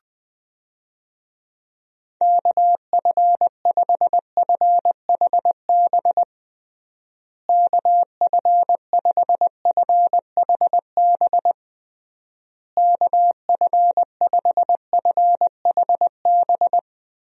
Morse code, heard as a keyed tone, about 20 wpm, 700 Hz. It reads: KF5FHB KF5FHB KF5FHB
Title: USA Call Signs - 6 Characters - Repeated 3x - 20wpm